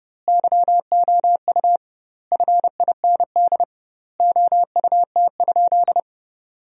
Morse code, heard as a keyed tone, about 32 wpm, 700 Hz. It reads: YOU FIND OUT?
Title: 2x Word-Spacing - Sentences from Top 100 Words - 30wpm